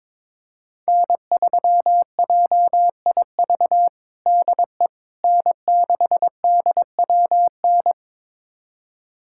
Morse code, heard as a keyed tone, about 21 wpm, 700 Hz. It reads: N3JIV DE N6DWN
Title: USA Callsign DE Callsign - 5 Characters - 22wpm